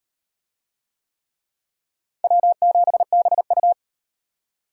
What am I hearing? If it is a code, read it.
Morse code, heard as a keyed tone, about 38 wpm, 700 Hz. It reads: W7BU